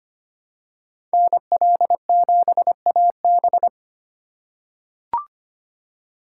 Morse code, heard as a keyed tone, about 25 wpm, 700 Hz. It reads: NL7AB
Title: USA Call Signs - 4x Character Spacing - 25wpm